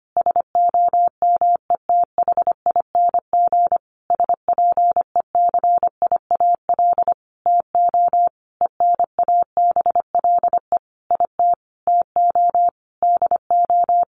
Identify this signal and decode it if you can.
Morse code, heard as a keyed tone, about 25 wpm, 700 Hz. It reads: SOMETHING SPECIAL TO ENABLE IT TO DO
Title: Sentences from Top 1500 Words 25wpm